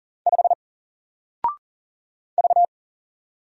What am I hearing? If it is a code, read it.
Morse code, heard as a keyed tone, about 40 wpm, 700 Hz. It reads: 5 V
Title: Single Letter-Number Alt 40wpm